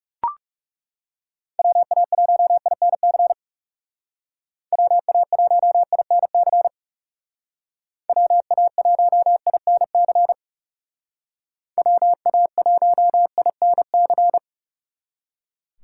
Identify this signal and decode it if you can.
Morse code, heard as a keyed tone, about 37 wpm, 700 Hz. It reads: WA1INC WA1INC WA1INC WA1INC
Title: Speed Racing - USA 2x3 Call Signs 30wpm